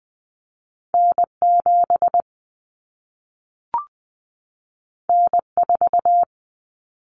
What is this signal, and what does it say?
Morse code, heard as a keyed tone, about 20 wpm, 700 Hz. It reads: N7 N4
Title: Common Cut Numbers - 2-Digit 20wpm